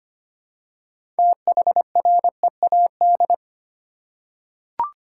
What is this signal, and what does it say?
Morse code, heard as a keyed tone, about 25 wpm, 690 Hz. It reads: THREAD
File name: practice / 25wpm / Easy Words with Hints - 25wpm